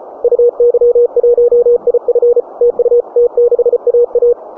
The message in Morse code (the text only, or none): UY1IF TU T6AA